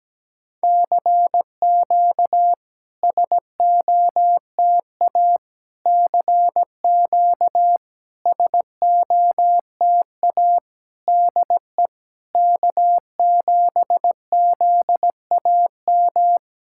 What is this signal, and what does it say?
Morse code, heard as a keyed tone, about 17 wpm, 700 Hz. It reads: CQ SOTA CQ SOTA DE K7ZAM
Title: SOTA QSO Practice 17wpm